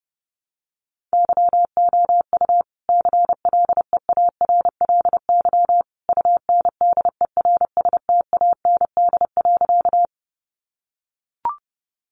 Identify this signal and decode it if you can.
Morse code, heard as a keyed tone, about 30 wpm, 700 Hz. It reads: YOU CLEARLY UNDERSTAND.
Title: Sentences from Top 1300 Words 30wpm